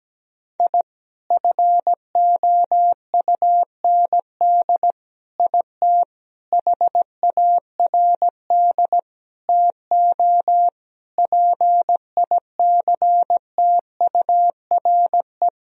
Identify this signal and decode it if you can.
Morse code, heard as a keyed tone, about 17 wpm, 700 Hz. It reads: I FOUND IT HARD TO PICTURE
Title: Sentences from Top 1000 Words - Review Part 1 - 17wpm